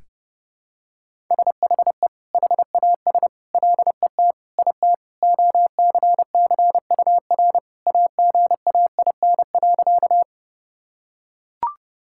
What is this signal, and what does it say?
Morse code, heard as a keyed tone, about 28 wpm, 700 Hz. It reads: SHE HAS LET IT OCCUR AGAIN.
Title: Sentences from Top 1000 Words - Review Part 6 - 30wpm